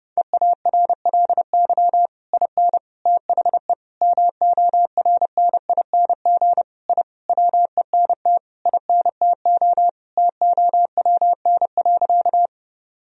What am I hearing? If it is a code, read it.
Morse code, heard as a keyed tone, about 30 wpm, 700 Hz. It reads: EARLY IN THE MORNING I WENT INTO TOWN.
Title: Sentences from Top 1000 Words - Review Part 2 - 30wpm